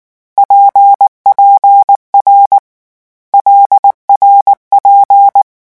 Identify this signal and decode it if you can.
Morse code, heard as a keyed tone, about 19 wpm, 800 Hz. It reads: PPR LRP
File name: morse mp3 / lecon6-vitesse 18